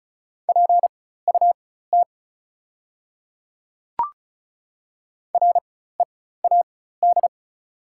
Morse code, heard as a keyed tone, about 36 wpm, spaced as for 16 wpm, 700 Hz. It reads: PUT READ
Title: 4x Character Spacing - Top 200 Words - 35wpm